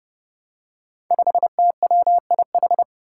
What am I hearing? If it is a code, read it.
Morse code, heard as a keyed tone, about 32 wpm, 700 Hz. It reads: 5TWIH
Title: Five Challenging Letter-Number Combinations 30wpm